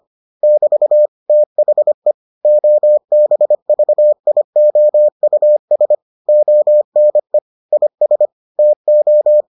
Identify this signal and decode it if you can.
Morse code, heard as a keyed tone, about 25 wpm, 600 Hz. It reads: <BT> THE OBVIOUS ONE IS TO